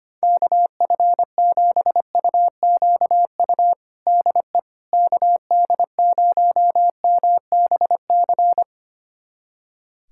Morse code, heard as a keyed tone, about 25 wpm, 700 Hz. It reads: KF7UQU DE KD0MBC
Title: USA Callsign DE Callsign - 6 Characters - 25wpm